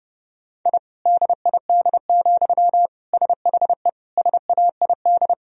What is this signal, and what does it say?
Morse code, heard as a keyed tone, about 30 wpm, 700 Hz. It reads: I DID, SHE SAID
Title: Sentences from Top 1000 Words - Review Part 4 - 30wpm